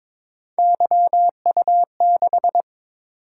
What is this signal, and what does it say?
Morse code, heard as a keyed tone, about 22 wpm, 700 Hz. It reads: YU6
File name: practice / 22wpm / Three Letter-Number Combinations - 22wpm